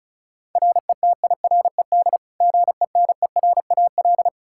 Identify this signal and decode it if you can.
Morse code, heard as a keyed tone, about 34 wpm, 700 Hz. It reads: RETIRED GENERAL
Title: Practice QSO - Exchange 3 - More Info - 35wpm